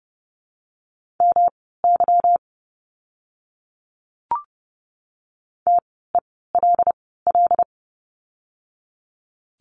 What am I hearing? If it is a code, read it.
Morse code, heard as a keyed tone, about 30 wpm, spaced as for 17 wpm, 700 Hz. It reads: MY TELL
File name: course / 30:10wpm / 237 Lesson 38f - Letter q - Words 30wpm